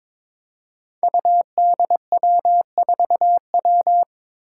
Morse code, heard as a keyed tone, about 22 wpm, 700 Hz. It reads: UDW4W
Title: Five Challenging Letter-Number Combinations 22wpm